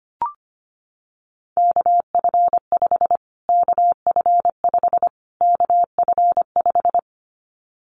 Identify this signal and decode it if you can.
Morse code, heard as a keyed tone, about 25 wpm, 700 Hz. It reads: KF5 KF5 KF5
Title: Callsign Prefixes for Top 25 DXCC - Repeated 3x - 25wpm